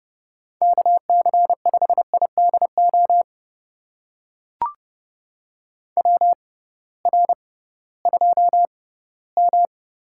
Morse code, heard as a keyed tone, about 30 wpm, spaced as for 18 wpm, 700 Hz. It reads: KC5IDO WR2M
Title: USA Call Signs - 6x Character Spacing - 30wpm